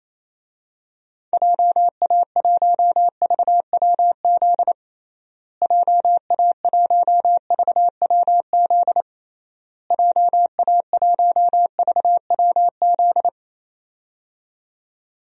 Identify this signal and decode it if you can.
Morse code, heard as a keyed tone, about 28 wpm, 700 Hz. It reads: JA1VWZ JA1VWZ JA1VWZ
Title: DXCC-100 Call Signs - 6 Characters - Repeated 3x - 28wpm